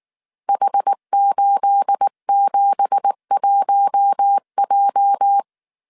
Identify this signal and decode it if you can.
Morse code, heard as a keyed tone, about 19 wpm, 800 Hz. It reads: H871J
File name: CW / DET / 1 9 1500C S C 18wpm 2539